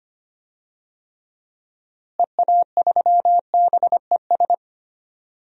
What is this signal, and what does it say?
Morse code, heard as a keyed tone, about 25 wpm, 700 Hz. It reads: EA3BES
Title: DXCC-100 Call Signs - 6 Characters - Repeated 3x - 25wpm